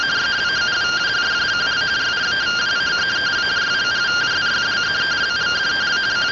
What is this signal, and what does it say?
Two-tone teleprinter signal, about 170 Hz shift.